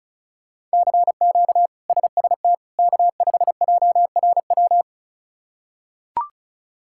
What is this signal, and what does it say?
Morse code, heard as a keyed tone, about 34 wpm, 700 Hz. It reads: CQ SST K5JRW
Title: K1USN SST QSO Practice - 35wpm